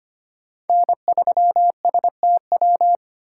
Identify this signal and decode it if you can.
Morse code, heard as a keyed tone, about 25 wpm, 700 Hz. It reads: N3STW
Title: USA Call Signs - 5 Characters - 6x Character Spacing - 25wpm